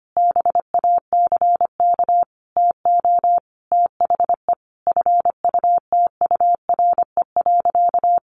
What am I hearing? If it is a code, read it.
Morse code, heard as a keyed tone, about 25 wpm, 700 Hz. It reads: BACK TO THE FUTURE.
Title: A Sound Of Thunder 25wpm